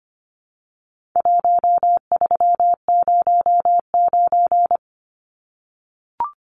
Morse code, heard as a keyed tone, about 25 wpm, 700 Hz. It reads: 1309
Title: Numbers - 4-Digit 25wpm